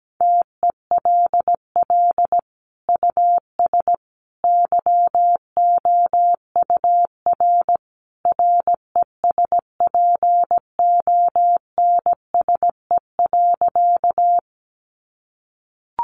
Morse code, heard as a keyed tone, about 17 wpm, 700 Hz. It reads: TELL US YOUR RESPONSE.
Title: Sentences from Top 1400 Words 17wpm